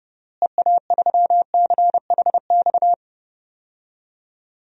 Morse code, heard as a keyed tone, about 32 wpm, 700 Hz. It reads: EA3CHX
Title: DXCC-100 Call Signs - 6 Characters - Repeated 2x - 30wpm